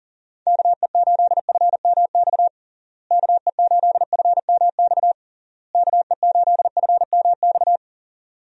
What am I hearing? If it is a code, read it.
Morse code, heard as a keyed tone, about 40 wpm, 700 Hz. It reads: KE8FMX KE8FMX KE8FMX
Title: USA Call Signs - 6 Characters - Repeated 3x - 40wpm